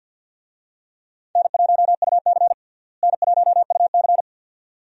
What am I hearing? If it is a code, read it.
Morse code, heard as a keyed tone, about 46 wpm, 700 Hz. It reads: N1UC N1UC
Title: USA Call Signs - 4 Characters - Repeated 2x - 50wpm